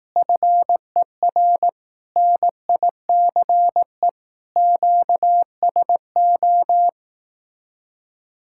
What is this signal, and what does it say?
Morse code, heard as a keyed tone, about 18 wpm, 700 Hz. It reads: FER NICE QSO
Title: Speed-Racing Common QSO Phrases Encore - 15wpm